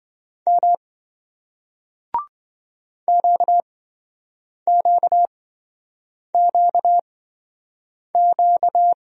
Morse code, heard as a keyed tone, about 27 wpm, spaced as for 7 wpm, 700 Hz. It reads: M QQQQ